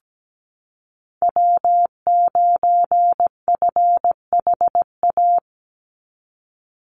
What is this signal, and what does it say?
Morse code, heard as a keyed tone, about 17 wpm, 700 Hz. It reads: W9FHA
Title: NA SOTA Chaser Callsigns 17wpm